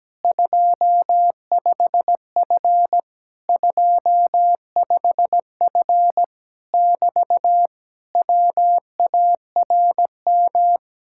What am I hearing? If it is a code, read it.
Morse code, heard as a keyed tone, about 17 wpm, 700 Hz. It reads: 25F 25F <BT> WARM
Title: Speed-Racing Common QSO Phrases - 17wpm